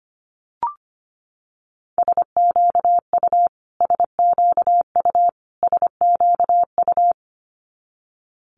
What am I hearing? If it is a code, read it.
Morse code, heard as a keyed tone, about 25 wpm, 700 Hz. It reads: SQU SQU SQU
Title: Multi-letter Phonemes Repeated 3x 25wpm